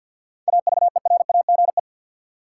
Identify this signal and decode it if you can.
Morse code, heard as a keyed tone, about 53 wpm, 700 Hz. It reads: AVERAGE
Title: Sentences from Top 1400 Words - New Word Drill Alt - 50wpm